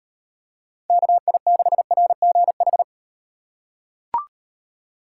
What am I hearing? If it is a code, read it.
Morse code, heard as a keyed tone, about 35 wpm, 700 Hz. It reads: KI6RGH